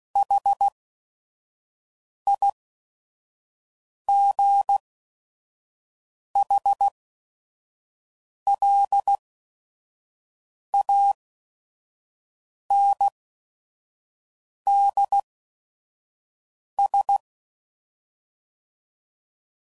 Morse code, heard as a keyed tone, about 16 wpm, spaced as for 5 wpm, 800 Hz. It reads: HIGHLANDS
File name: CW Code Groups @5WPM / fromSM's / Cities